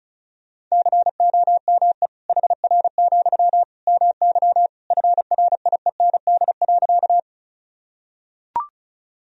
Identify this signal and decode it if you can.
Morse code, heard as a keyed tone, about 34 wpm, 700 Hz. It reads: COME HR, MY FRIEND.
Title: Sentences from Top 300 Words and Abbreviations 35wpm